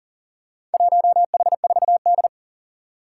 Morse code, heard as a keyed tone, about 40 wpm, 700 Hz. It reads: 1H4D
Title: Four Challenging Letter-Number Combinations 40wpm